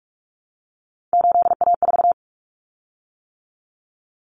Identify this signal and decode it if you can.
Morse code, heard as a keyed tone, about 44 wpm, 700 Hz. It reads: 8A4
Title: Common Cut Numbers - 3-Digit 45wpm